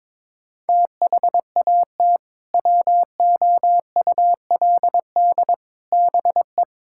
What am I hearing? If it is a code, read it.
Morse code, heard as a keyed tone, about 22 wpm, 700 Hz. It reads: THAT WOULD BE
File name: practice / 22wpm / Sentences with 2-Word Phrases - 22wpm